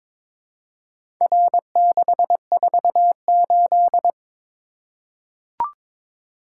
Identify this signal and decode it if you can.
Morse code, heard as a keyed tone, about 22 wpm, 690 Hz. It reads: R648
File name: practice / 22wpm / Four Letter-Number Combinations - 22wpm